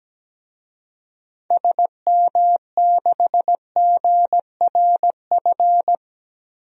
Morse code, heard as a keyed tone, about 17 wpm, 700 Hz. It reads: SM6GRF
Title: DXCC-100 Call Signs - 6 Characters - Repeated 2x - 17wpm